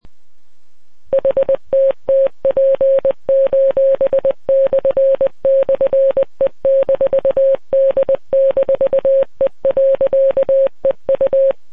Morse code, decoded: HTTP://E-D-E.EU